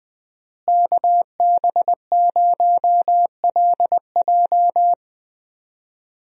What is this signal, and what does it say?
Morse code, heard as a keyed tone, about 20 wpm, 700 Hz. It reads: KB0LJ
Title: USA Call Signs - 5 Characters - Repeated 2x - 20wpm